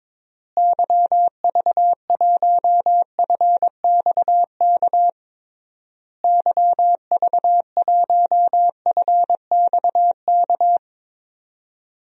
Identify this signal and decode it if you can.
Morse code, heard as a keyed tone, about 22 wpm, 700 Hz. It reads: YV1FXK YV1FXK
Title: DXCC-100 Call Signs - 6 Characters - Repeated 3x - 22wpm